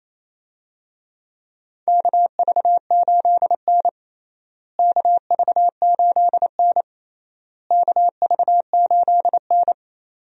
Morse code, heard as a keyed tone, about 28 wpm, 700 Hz. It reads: KV8N KV8N KV8N
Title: USA Call Signs - 4 Characters - Repeated 3x - 28wpm